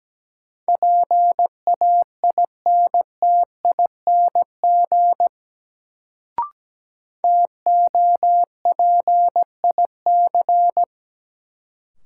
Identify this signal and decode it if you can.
Morse code, heard as a keyed tone, about 17 wpm, 700 Hz. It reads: PAINTING TOPIC
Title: Sentences from Top 1200 Words - New Word Drill Alt - 17wpm